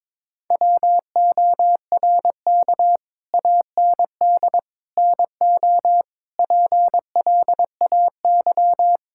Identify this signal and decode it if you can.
Morse code, heard as a keyed tone, about 22 wpm, 700 Hz. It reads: WORK AND NO PLAY
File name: practice / 22wpm / Sets of 7 Words - 22wpm